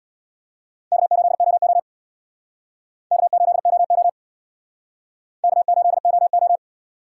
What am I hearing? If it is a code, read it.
Morse code, heard as a keyed tone, about 64 wpm, 700 Hz. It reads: K8OO K8OO K8OO